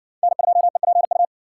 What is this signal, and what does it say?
Morse code, heard as a keyed tone, about 56 wpm, 700 Hz. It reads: N2EPU